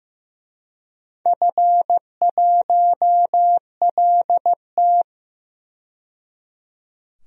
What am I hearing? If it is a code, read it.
Morse code, heard as a keyed tone, about 15 wpm, 700 Hz. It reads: F1LT
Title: DXCC-100 Call Signs - 4 Characters - 15wpm